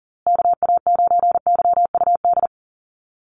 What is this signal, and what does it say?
Morse code, heard as a keyed tone, about 40 wpm, 700 Hz. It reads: KA9YUD
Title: USA Call Signs Repeated 3x 40wpm